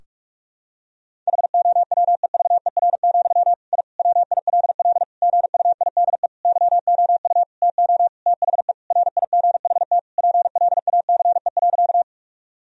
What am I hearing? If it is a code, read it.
Morse code, heard as a keyed tone, about 46 wpm, 700 Hz. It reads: HOWEVER, I WILL GUIDE YOU TO THE RIGHT PLACE.
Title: Sentences from Top 1000 Words - Review Part 3 - 45wpm